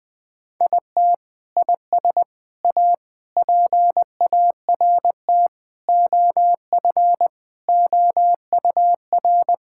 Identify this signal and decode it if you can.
Morse code, heard as a keyed tone, about 21 wpm, 700 Hz. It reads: IT IS A PART OF OUR
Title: Sentences from Top 1000 Words - Review Part 5 - 20wpm